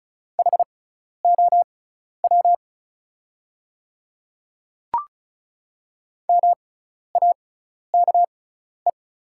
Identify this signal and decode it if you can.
Morse code, heard as a keyed tone, about 34 wpm, spaced as for 12 wpm, 700 Hz. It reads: HOW MAKE
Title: 6x Character Spacing - Top 100 Words - 35wpm